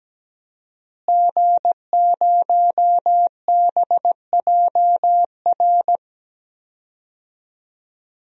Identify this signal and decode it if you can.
Morse code, heard as a keyed tone, about 17 wpm, 700 Hz. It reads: G0BJR
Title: DXCC-100 Call Signs - 5 Characters - Repeated 2x - 17wpm